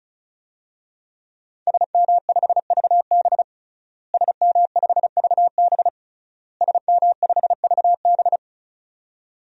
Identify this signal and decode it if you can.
Morse code, heard as a keyed tone, about 36 wpm, 700 Hz. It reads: SM5VB SM5VB SM5VB
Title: DXCC-100 Call Signs - 5 Characters - Repeated 3x - 35wpm